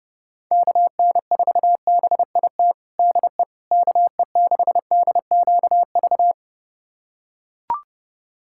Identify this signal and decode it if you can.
Morse code, heard as a keyed tone, about 28 wpm, 700 Hz. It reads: KN4BIT DE KE6DQV
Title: USA Callsign DE Callsign - 6 Characters - 30wpm